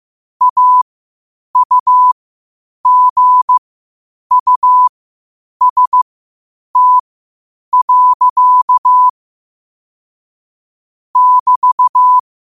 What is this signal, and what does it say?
Morse code, heard as a keyed tone, about 15 wpm, spaced as for 8 wpm, 1000 Hz. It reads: AUGUST. <BT>
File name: 8wpm